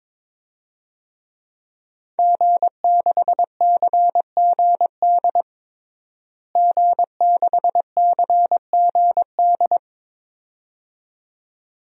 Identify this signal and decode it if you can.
Morse code, heard as a keyed tone, about 22 wpm, 700 Hz. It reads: G6CGD G6CGD
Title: DXCC-100 Call Signs - 5 Characters - Repeated 2x - 22wpm